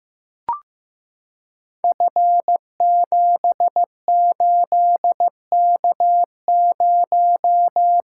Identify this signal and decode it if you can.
Morse code, heard as a keyed tone, about 15 wpm, 700 Hz. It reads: F78K0
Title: Code Groups - 5 Letter:Numbers - 15wpm